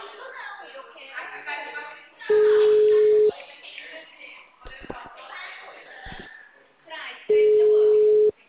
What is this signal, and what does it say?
Ringback tone (Europe, 425 Hz), with speech.